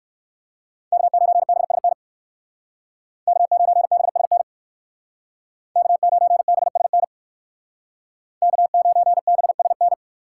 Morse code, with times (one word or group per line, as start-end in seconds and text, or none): 0.9-1.9 K9BSN
3.3-4.4 K9BSN
5.8-7.0 K9BSN
8.4-9.9 K9BSN